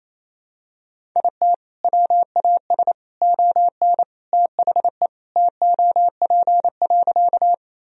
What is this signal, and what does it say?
Morse code, heard as a keyed tone, about 28 wpm, 700 Hz. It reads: IT WAS ON THE TOP.
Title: Sentences from Top 1000 Words - Review Part 2 - 28wpm